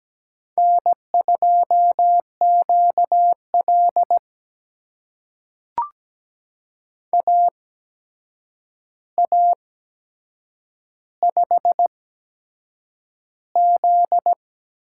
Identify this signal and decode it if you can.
Morse code, heard as a keyed tone, about 17 wpm, spaced as for 7 wpm, 700 Hz. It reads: N2QL AA5Z
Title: USA Call Signs - 4 Characters - 8x Character Spacing - 17wpm